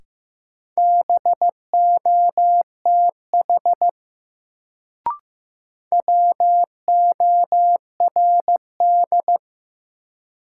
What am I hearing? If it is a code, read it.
Morse code, heard as a keyed tone, about 15 wpm, 700 Hz. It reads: BOTH WORD
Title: Sentences from Top 300 Words - New Word Drill - 15wpm